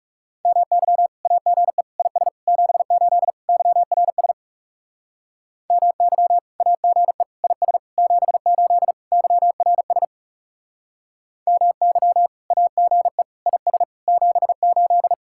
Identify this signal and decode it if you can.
Morse code, heard as a keyed tone, about 39 wpm, 700 Hz. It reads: MY AGE IS 78 YRS MY AGE IS 78 YRS MY AGE IS 78